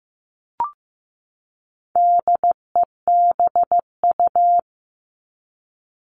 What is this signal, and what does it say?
Morse code, heard as a keyed tone, about 15 wpm, 700 Hz. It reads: DEBU
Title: All Cut Numbers - 4-Digit 15wpm